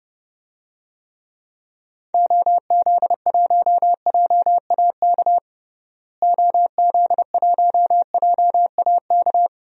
Morse code, heard as a keyed tone, about 30 wpm, 700 Hz. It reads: OZ1JAK OZ1JAK
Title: DXCC-100 Call Signs - 6 Characters - Repeated 2x - 30wpm